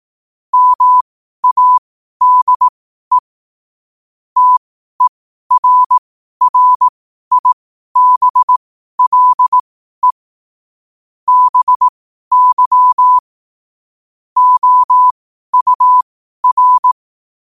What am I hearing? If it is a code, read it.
Morse code, heard as a keyed tone, about 18 wpm, spaced as for 12 wpm, 1000 Hz. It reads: MADE TERRIBLE BY OUR
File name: qotdfiles / 12wpm